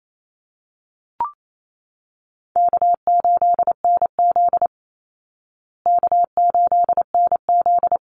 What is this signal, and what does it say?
Morse code, heard as a keyed tone, about 28 wpm, 700 Hz. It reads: K8NZ K8NZ
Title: Call Signs Repeated 3x - 28wpm